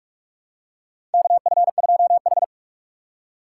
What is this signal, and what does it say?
Morse code, heard as a keyed tone, about 44 wpm, 700 Hz. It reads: KF2H
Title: USA Call Signs - 4 Characters - 3x Character Spacing - 45wpm